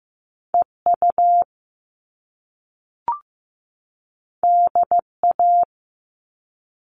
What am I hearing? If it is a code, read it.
Morse code, heard as a keyed tone, about 15 wpm, 700 Hz. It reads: EU DA
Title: All Cut Numbers - 2-Digit 15wpm